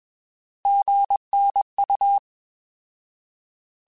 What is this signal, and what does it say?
Morse code, heard as a keyed tone, about 21 wpm, 800 Hz. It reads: GNU
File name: Alphabetical / Alphabetical mp3s / Gs @20